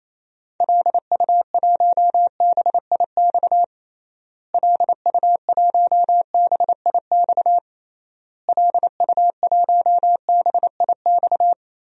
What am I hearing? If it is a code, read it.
Morse code, heard as a keyed tone, about 28 wpm, 700 Hz. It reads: LU1BIX LU1BIX LU1BIX
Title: DXCC-100 Call Signs - 6 Characters - Repeated 3x - 28wpm